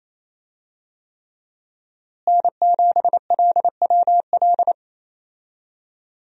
Morse code, heard as a keyed tone, about 27 wpm, 700 Hz. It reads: N7LWL
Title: USA Call Signs - 5 Characters - Repeated 3x - 28wpm